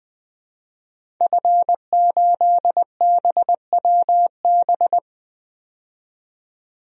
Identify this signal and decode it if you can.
Morse code, heard as a keyed tone, about 20 wpm, 700 Hz. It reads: F8BWB
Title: DXCC-100 Call Signs - 5 Characters - 20wpm